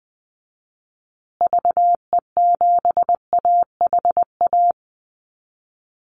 Morse code, heard as a keyed tone, about 20 wpm, 700 Hz. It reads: VE7AHA
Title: Call Signs Repeated 2x - 20wpm